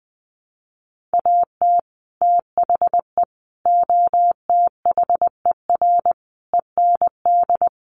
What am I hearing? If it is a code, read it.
Morse code, heard as a keyed tone, about 20 wpm, 700 Hz. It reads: AT THE OTHER END
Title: Sets of 5 Words 20wpm